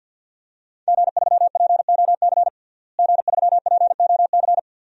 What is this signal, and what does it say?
Morse code, heard as a keyed tone, about 53 wpm, 700 Hz. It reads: G3POC G3POC
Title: DXCC-100 Call Signs - 5 Characters - Repeated 3x - 50wpm